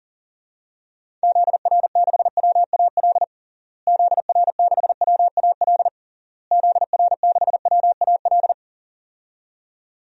Morse code, heard as a keyed tone, about 40 wpm, 700 Hz. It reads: ZR6WAL ZR6WAL ZR6WAL
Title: DXCC-100 Call Signs - 6 Characters - Repeated 3x - 40wpm